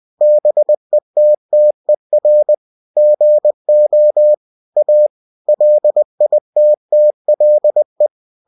Morse code, heard as a keyed tone, about 20 wpm, 600 Hz. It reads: BETTER GO A LITTLE
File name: cw-books / winnie-the-pooh / winnie-20wpm-page-0037